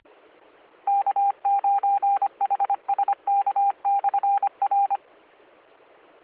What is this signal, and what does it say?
Morse code, heard as a keyed tone, about 25 wpm, 800 Hz. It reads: K9HSK/R